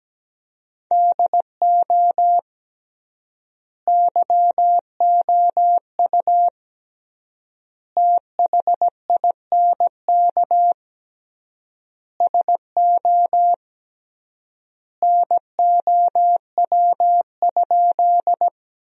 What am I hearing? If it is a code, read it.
Morse code, heard as a keyed tone, about 17 wpm, 700 Hz. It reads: DO YOU THINK SO NOW?
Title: 3x Word-Spacing - Sentences from Top 200 Words - 17wpm